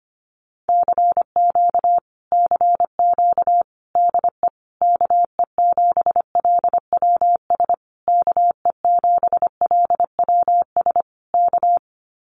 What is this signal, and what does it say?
Morse code, heard as a keyed tone, about 25 wpm, 700 Hz. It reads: CQ CQ DE KE7LWS KE7LWS K
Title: Common QSO Phrases Encore 25wpm